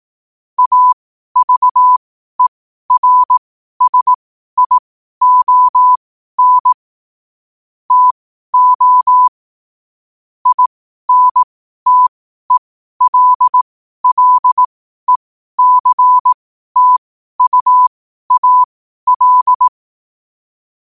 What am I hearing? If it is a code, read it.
Morse code, heard as a keyed tone, about 18 wpm, spaced as for 12 wpm, 1000 Hz. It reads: AVERSION TO INTELLECTUAL
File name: qotdfiles / 12wpm